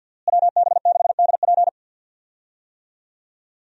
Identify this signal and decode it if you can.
Morse code, heard as a keyed tone, about 51 wpm, 700 Hz. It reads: WB6DP
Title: USA Call Signs - 5 Characters - Repeated 3x - 50wpm